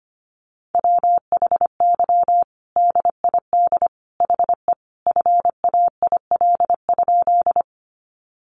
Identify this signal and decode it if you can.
Morse code, heard as a keyed tone, about 25 wpm, 700 Hz. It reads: WHY DID HE FAIL?
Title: Sentences from Top 1100 Words 25wpm